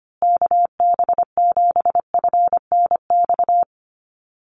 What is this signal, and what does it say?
Morse code, heard as a keyed tone, about 25 wpm, 700 Hz. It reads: KB7FNX
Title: USA Call Signs Repeated 2x 25wpm